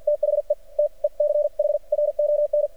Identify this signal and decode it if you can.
Morse code, heard as a keyed tone, about 48 wpm, spaced as for 40 wpm, 590 Hz. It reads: THE TECHRON
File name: archived / CRSnet